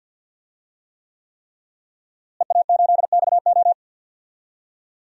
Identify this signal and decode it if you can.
Morse code, heard as a keyed tone, about 51 wpm, 700 Hz. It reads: EA8XO
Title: DXCC-100 Call Signs - 5 Characters - Repeated 3x - 50wpm